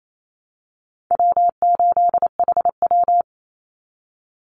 Morse code, heard as a keyed tone, about 28 wpm, 700 Hz. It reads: W8HW
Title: Easy Call Signs Repeated 3x - 28wpm